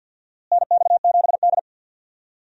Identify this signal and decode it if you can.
Morse code, heard as a keyed tone, about 51 wpm, 700 Hz. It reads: NX7D
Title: USA Call Signs - 4 Characters - Repeated 3x - 50wpm